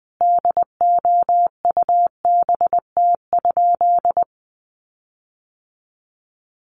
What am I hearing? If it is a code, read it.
Morse code, heard as a keyed tone, about 20 wpm, 700 Hz. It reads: DOUBT?
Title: Sentences from Top 1500 Words 20wpm